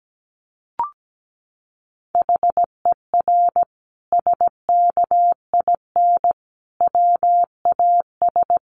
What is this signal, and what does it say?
Morse code, heard as a keyed tone, about 17 wpm, 700 Hz. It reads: HER SKIN WAS
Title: Sentences from Top 1300 Words 17wpm